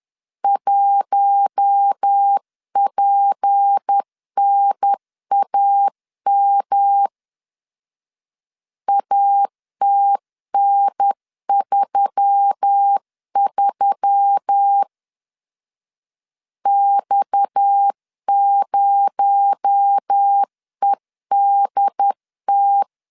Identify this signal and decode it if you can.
Morse code, heard as a keyed tone, about 10 wpm, 800 Hz. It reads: 1PNAM ATN33 X0EDT
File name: CW / DET / 1 6 1500C S C 10wpm 4252